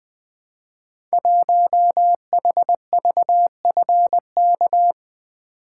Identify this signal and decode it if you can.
Morse code, heard as a keyed tone, about 20 wpm, 700 Hz. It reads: 1HVFK
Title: Code Groups - 5 Letter:Numbers - 20wpm